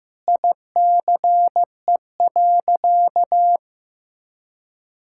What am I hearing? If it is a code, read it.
Morse code, heard as a keyed tone, about 15 wpm, 700 Hz. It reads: ICE.
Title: Sentences from Top 1600 Words 15wpm